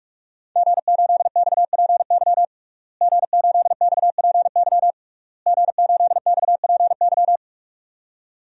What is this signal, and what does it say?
Morse code, heard as a keyed tone, about 43 wpm, 700 Hz. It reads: G8XPY G8XPY G8XPY
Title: DXCC-100 Call Signs - 5 Characters - Repeated 3x - 45wpm